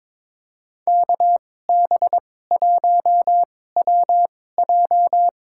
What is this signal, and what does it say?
Morse code, heard as a keyed tone, about 22 wpm, spaced as for 16 wpm, 700 Hz. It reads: KB1WJ
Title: USA Call Signs - 2x Character Spacing - 22wpm